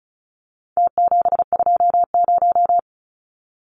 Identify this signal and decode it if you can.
Morse code, heard as a keyed tone, about 34 wpm, 700 Hz. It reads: T720
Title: Numbers - Contest Style - 4-Digit 35wpm